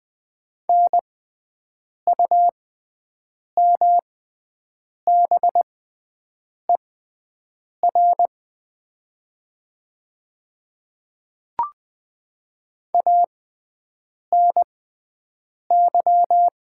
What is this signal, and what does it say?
Morse code, heard as a keyed tone, about 20 wpm, spaced as for 7 wpm, 700 Hz. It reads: NUMBER ANY
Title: 6x Character Spacing - Top 200 Words - 20wpm